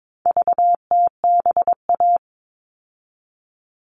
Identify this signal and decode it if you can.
Morse code, heard as a keyed tone, about 22 wpm, 700 Hz. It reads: VTBA
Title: All Cut Numbers - 4-Digit 22wpm